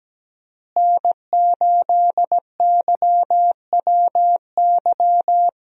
Morse code, heard as a keyed tone, about 17 wpm, 700 Hz. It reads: N8YWY
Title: USA Call Signs - 5 Characters - Repeated 3x - 17wpm